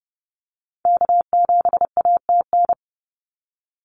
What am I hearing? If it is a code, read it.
Morse code, heard as a keyed tone, about 30 wpm, 700 Hz. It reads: K7ATN
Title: NA SOTA Activator Callsigns 30wpm